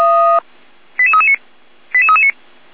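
Five-tone selective call (ZVEI), heard as quick runs of five tones, tones of about 70 ms. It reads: 88339 88339
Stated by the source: ZVEI five-tone call received on SDR.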